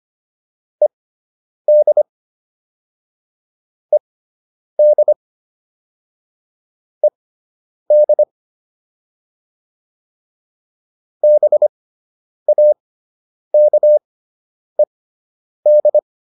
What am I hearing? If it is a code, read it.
Morse code, heard as a keyed tone, about 25 wpm, spaced as for 9 wpm, 600 Hz. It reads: ED ED ED BAKED